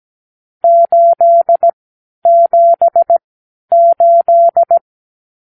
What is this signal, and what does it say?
Morse code, heard as a keyed tone, about 17 wpm, spaced as for 10 wpm, 680 Hz. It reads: 878
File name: k7qo / Lesson 54 (9)